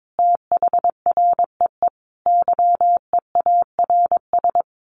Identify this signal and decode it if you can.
Morse code, heard as a keyed tone, about 22 wpm, 700 Hz. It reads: THREE YEARS
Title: Sets of 5 Words Encore2 22wpm